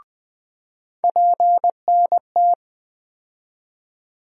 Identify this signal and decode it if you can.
Morse code, heard as a keyed tone, about 20 wpm, 700 Hz. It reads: PNT
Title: USA Call Sign Postfixes - 3 Letters 20wpm